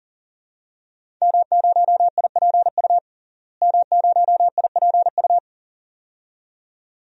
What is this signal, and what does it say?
Morse code, heard as a keyed tone, about 40 wpm, 700 Hz. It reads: M0IPU M0IPU
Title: DXCC-100 Call Signs - 5 Characters - Repeated 2x - 40wpm